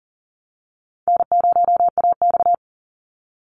Morse code, heard as a keyed tone, about 38 wpm, 700 Hz. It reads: N0AX
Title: Call Signs Repeated 2x - 40wpm